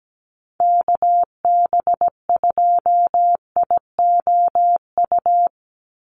Morse code, heard as a keyed tone, about 17 wpm, 700 Hz. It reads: KB2IOU
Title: USA Call Signs 17wpm